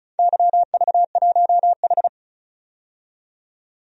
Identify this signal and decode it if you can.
Morse code, heard as a keyed tone, about 36 wpm, 690 Hz. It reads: YV1H
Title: DXCC-100 Call Signs - 4 Characters - 35wpm